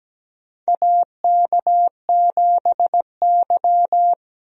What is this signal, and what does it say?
Morse code, heard as a keyed tone, about 17 wpm, 700 Hz. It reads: AK7Y
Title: USA Call Signs - 4 Characters - 17wpm